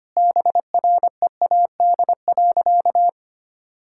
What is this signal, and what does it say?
Morse code, heard as a keyed tone, about 25 wpm, 700 Hz. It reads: BREAD.
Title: Sentences from Top 1600 Words 25wpm